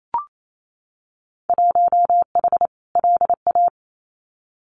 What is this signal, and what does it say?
Morse code, heard as a keyed tone, about 28 wpm, 700 Hz. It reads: 1H LA
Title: Winter Field Day Exchanges 28wpm